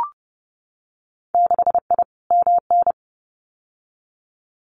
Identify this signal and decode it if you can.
Morse code, heard as a keyed tone, about 30 wpm, 700 Hz. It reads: 6I MN
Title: Winter Field Day Exchanges Encore 30wpm